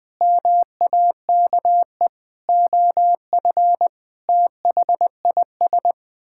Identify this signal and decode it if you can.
Morse code, heard as a keyed tone, about 20 wpm, 700 Hz. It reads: MAKE OF THIS
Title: Sentences with 5-Word Phrases - 20wpm